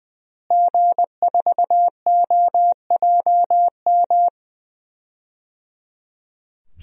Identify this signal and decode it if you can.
Morse code, heard as a keyed tone, about 20 wpm, 700 Hz. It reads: G4OJM
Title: DXCC-100 Call Signs - 5 Characters - Repeated 3x - 20wpm